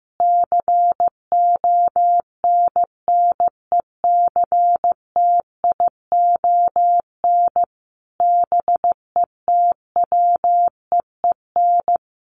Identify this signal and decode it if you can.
Morse code, heard as a keyed tone, about 15 wpm, 700 Hz. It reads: CONNECTION BETWEEN
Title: Sentences from Top 1300 Words 15wpm